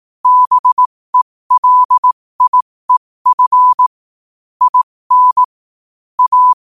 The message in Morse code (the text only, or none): BELIEF IN A